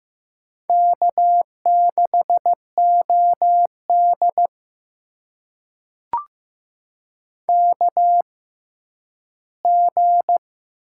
Morse code, heard as a keyed tone, about 15 wpm, spaced as for 10 wpm, 700 Hz. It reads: K6OD KG
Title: USA Call Signs - 6x Character Spacing - 15wpm